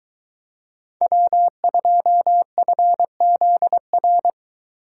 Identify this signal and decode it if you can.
Morse code, heard as a keyed tone, about 22 wpm, 700 Hz. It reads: W2FZR